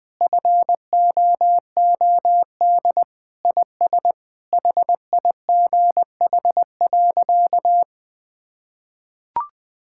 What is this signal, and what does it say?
Morse code, heard as a keyed tone, about 19 wpm, 700 Hz. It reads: FOOD IS HIGH.
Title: Sentences with 3-Word Phrases - Spoken Beginning - 20wpm